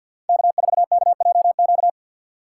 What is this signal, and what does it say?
Morse code, heard as a keyed tone, about 51 wpm, 700 Hz. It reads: K4KJQ
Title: USA Call Signs - 5 Characters - 4x Character Spacing - 50wpm